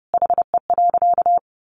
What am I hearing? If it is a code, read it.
Morse code, heard as a keyed tone, about 30 wpm, 700 Hz. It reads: HE.
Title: Sentences from Top 300 Words 30wpm